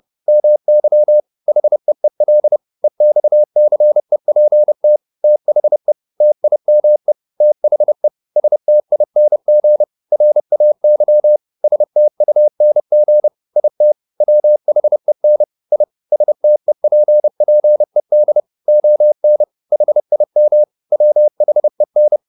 Morse code, heard as a keyed tone, about 30 wpm, 600 Hz. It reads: MY HEEL EXCEPT THE TIME THE STING RAY STUNG IT WHEN I STEPPED ON HIM WHEN